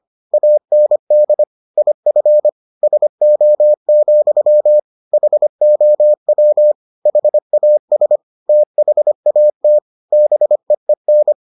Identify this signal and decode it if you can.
Morse code, heard as a keyed tone, about 25 wpm, 600 Hz. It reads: AND IF SO, HOW HAS THAT BEEN